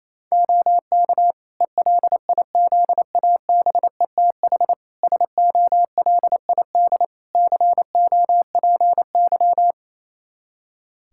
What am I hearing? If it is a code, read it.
Morse code, heard as a keyed tone, about 28 wpm, 700 Hz. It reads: OK ELIZABETH SOLID COPY
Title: Practice QSO - Exchange 3 - More Info - 28wpm